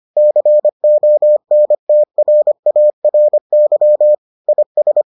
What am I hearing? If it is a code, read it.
Morse code, heard as a keyed tone, about 25 wpm, 600 Hz. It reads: CONTRARY IS